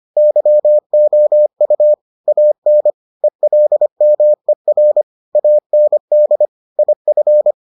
Morse code, heard as a keyed tone, about 25 wpm, 600 Hz. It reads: YOU AN ELMER AND IF